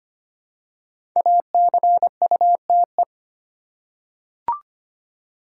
Morse code, heard as a keyed tone, about 25 wpm, 700 Hz. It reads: ACUTE
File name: 5 Letter Words - First 3 Letters Given - 25wpm